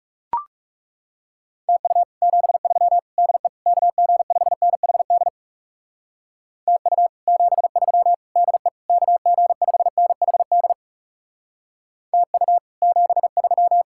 Morse code, heard as a keyed tone, about 37 wpm, 700 Hz. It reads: TU 73 DE KG5NHD TU 73 DE KG5NHD TU 73
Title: Speed-Racing Common QSO Phrases - 30wpm